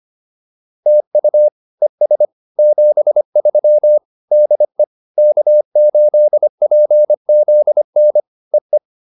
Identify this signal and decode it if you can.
Morse code, heard as a keyed tone, about 25 wpm, 600 Hz. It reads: TU ES 73 DE K8PZN EE
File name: POTA201 25